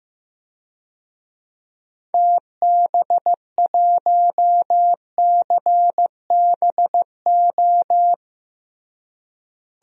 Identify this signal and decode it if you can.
Morse code, heard as a keyed tone, about 16 wpm, 700 Hz. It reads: TB1CBO
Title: DXCC-100 Call Signs - 6 Characters - Repeated 3x - 15wpm